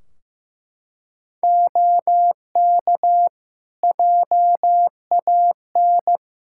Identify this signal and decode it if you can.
Morse code, heard as a keyed tone, about 15 wpm, 700 Hz. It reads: OK JAN
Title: K1USN SST QSO Practice 15wpm